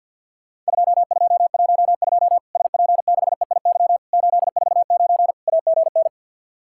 Morse code, heard as a keyed tone, about 50 wpm, 700 Hz. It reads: 2212 SP6EIY 849 AGN
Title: Worked All Europe DX Contest - QTC Exchange with Repeats - 50wpm